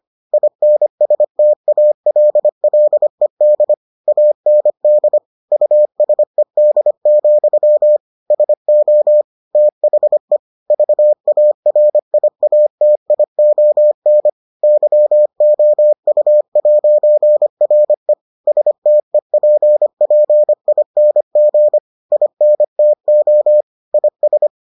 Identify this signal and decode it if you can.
Morse code, heard as a keyed tone, about 25 wpm, 600 Hz. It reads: INSTALLED AND USED, SO THE VARIATION YOU'RE STEPPING INTO IS